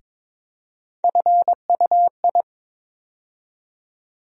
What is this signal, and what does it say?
Morse code, heard as a keyed tone, about 22 wpm, 700 Hz. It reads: FUI
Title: USA Call Sign Postfixes - 3 Letters 22wpm